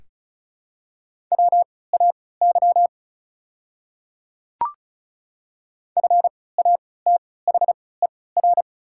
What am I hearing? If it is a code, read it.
Morse code, heard as a keyed tone, about 34 wpm, spaced as for 20 wpm, 700 Hz. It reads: WAY FATHER
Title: 3x Character Spacing - Top 300 Words - 35wpm